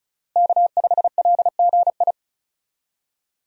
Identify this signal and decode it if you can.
Morse code, heard as a keyed tone, about 34 wpm, 700 Hz. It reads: K5LGI
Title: USA Call Signs - 5 Characters - Repeated 3x - 35wpm